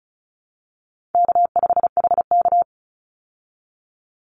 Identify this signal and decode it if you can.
Morse code, heard as a keyed tone, about 34 wpm, 700 Hz. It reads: K5HK
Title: Call Signs Repeated 2x - 35wpm